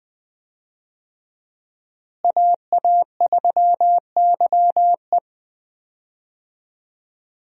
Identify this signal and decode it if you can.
Morse code, heard as a keyed tone, about 21 wpm, 700 Hz. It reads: AA3YE
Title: USA Call Signs - 5 Characters - Repeated 3x - 20wpm